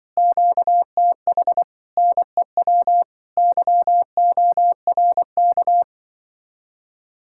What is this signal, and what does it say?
Morse code, heard as a keyed tone, about 24 wpm, 700 Hz. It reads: QTH NEW YORK
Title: Speed-Racing Common QSO Phrases - 20wpm